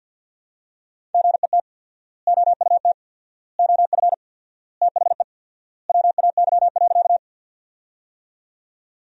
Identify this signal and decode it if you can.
Morse code, heard as a keyed tone, about 50 wpm, 700 Hz. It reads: GET OUT OF THE WAY.
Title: 4x Word-Spacing - Sentences from Top 200 Words - 50wpm